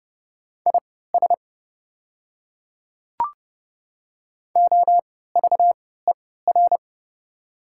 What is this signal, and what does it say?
Morse code, heard as a keyed tone, about 30 wpm, spaced as for 17 wpm, 700 Hz. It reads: IS OVER
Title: 3x Character Spacing - Top 100 Words - 30wpm